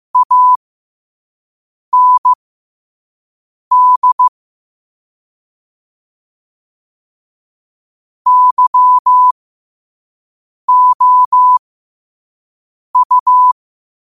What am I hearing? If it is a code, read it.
Morse code, heard as a keyed tone, about 15 wpm, spaced as for 5 wpm, 1000 Hz. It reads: AND YOU
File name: qotdfiles / 5wpm